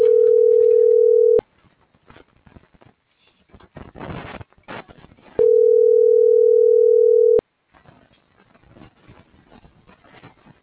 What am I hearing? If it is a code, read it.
Ringback tone (North America), 440 + 480 Hz.